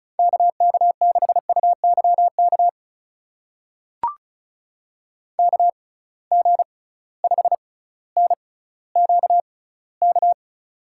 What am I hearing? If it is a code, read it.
Morse code, heard as a keyed tone, about 36 wpm, spaced as for 18 wpm, 700 Hz. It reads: KK6UYK KG5NQK
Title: USA Call Signs - 6x Character Spacing - 35wpm